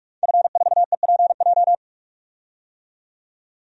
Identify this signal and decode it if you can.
Morse code, heard as a keyed tone, about 46 wpm, 700 Hz. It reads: F4EPJ
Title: DXCC-100 Call Signs - 5 Characters - 45wpm